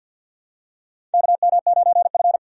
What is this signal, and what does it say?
Morse code, heard as a keyed tone, about 50 wpm, 700 Hz. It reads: KM9F
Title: USA Call Signs - 4 Characters - 50wpm